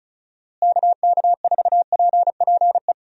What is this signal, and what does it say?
Morse code, heard as a keyed tone, about 34 wpm, 700 Hz. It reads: KK4PPE
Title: USA Call Signs - 6 Characters - Repeated 3x - 35wpm